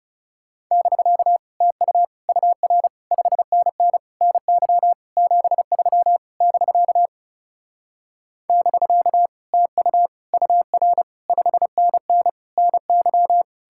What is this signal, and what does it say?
Morse code, heard as a keyed tone, about 40 wpm, 700 Hz. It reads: <BK> TU UR 5NN NY 73 <BK> <BK> TU UR 5NN NY